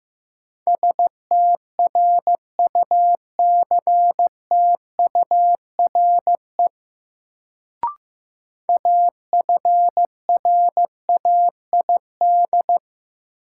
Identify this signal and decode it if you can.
Morse code, heard as a keyed tone, about 15 wpm, 700 Hz. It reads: STRUCTURE AFRAID
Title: Sentences from Top 1500 Words - New Word Drill Alt - 15wpm